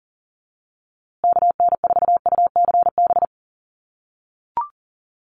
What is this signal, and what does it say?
Morse code, heard as a keyed tone, about 40 wpm, 700 Hz. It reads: KN4UCB
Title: USA Call Signs Repeated 2x 40wpm